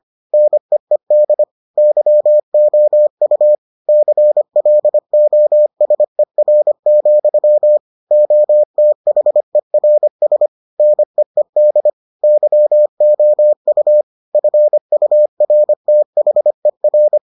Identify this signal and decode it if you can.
Morse code, heard as a keyed tone, about 25 wpm, 600 Hz. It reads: NEED YOU CLOSER, OTHERS NEED YOU FURTHER